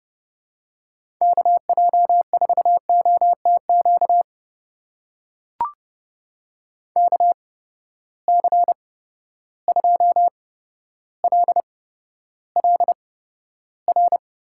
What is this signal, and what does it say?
Morse code, heard as a keyed tone, about 30 wpm, spaced as for 13 wpm, 700 Hz. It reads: KJ4OTQ KC2LLR
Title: USA Call Signs - 6 Characters - 8x Character Spacing - 30wpm